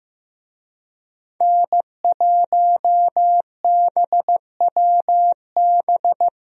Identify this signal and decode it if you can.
Morse code, heard as a keyed tone, about 15 wpm, 700 Hz. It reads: N1BWB
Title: USA Call Signs - 5 Characters - 15wpm